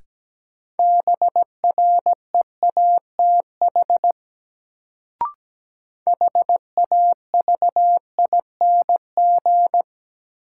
Morse code, heard as a keyed tone, about 17 wpm, 700 Hz. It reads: BREATH HAVING
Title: Sets of 5 Words Encore - New Word Drill - 17wpm